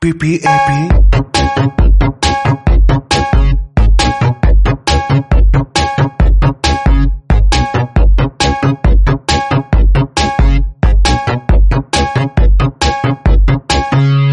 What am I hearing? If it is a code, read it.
Speech. There is no code.